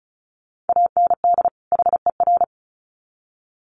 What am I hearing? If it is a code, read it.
Morse code, heard as a keyed tone, about 34 wpm, 700 Hz. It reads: AND HER
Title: Sets of 2 Words 35wpm